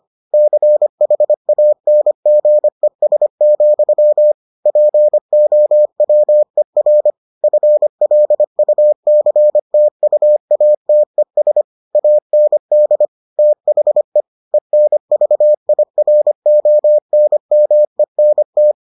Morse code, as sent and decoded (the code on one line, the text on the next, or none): -.-. .... .- -. --. ... --..-- / .--. --- .-- . .-. / ..-. .-.. ..- -.-. - ..- .- - . ... / .- -. -.. / - .... / . -. ...- .. .-. --- -. -- . -. -
CHANGES, POWER FLUCTUATES AND THE ENVIRONMENT